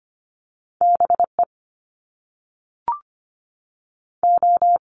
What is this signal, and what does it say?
Morse code, heard as a keyed tone, about 25 wpm, 700 Hz. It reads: BE O
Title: Top Two Letter Combinations in Words 25wpm